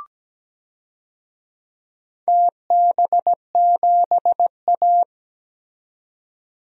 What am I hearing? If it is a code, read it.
Morse code, heard as a keyed tone, about 17 wpm, 700 Hz. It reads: TB7A
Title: DXCC-100 Call Signs - 4 Characters - 17wpm